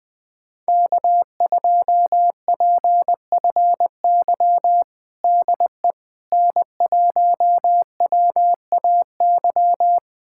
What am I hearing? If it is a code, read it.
Morse code, heard as a keyed tone, about 21 wpm, 700 Hz. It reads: K2PFY DE N1WAY
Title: USA Callsign DE Callsign 20wpm